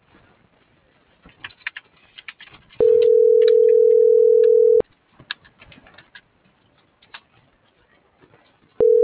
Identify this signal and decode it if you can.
Ringback tone (North America), 440 + 480 Hz.